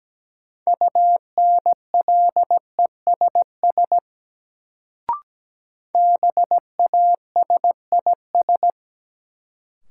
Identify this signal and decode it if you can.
Morse code, heard as a keyed tone, about 17 wpm, 700 Hz. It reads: UNLESS BASIS
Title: Sentences from Top 1000 Words - Review Part 5 - New Word Drill - 17wpm